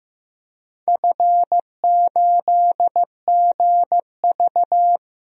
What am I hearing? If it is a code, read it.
Morse code, heard as a keyed tone, about 15 wpm, 700 Hz. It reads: F8GV
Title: DXCC-100 Call Signs - 4 Characters - Repeated 2x - 15wpm